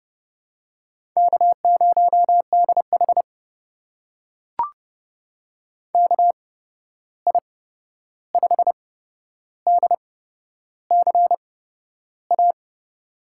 Morse code, heard as a keyed tone, about 30 wpm, spaced as for 11 wpm, 700 Hz. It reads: K0DH KI5DCA
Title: USA Call Signs - 8x Character Spacing - 30wpm